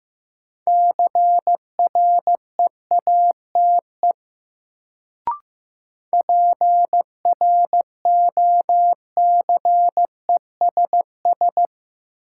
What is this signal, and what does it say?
Morse code, heard as a keyed tone, about 15 wpm, 700 Hz. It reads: CREATE PROCESS
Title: Sentences from Top 1000 Words - Review Part 3 - New Word Drill - 15wpm